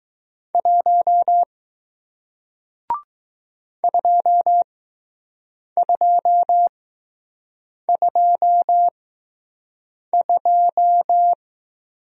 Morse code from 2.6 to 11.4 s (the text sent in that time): 2222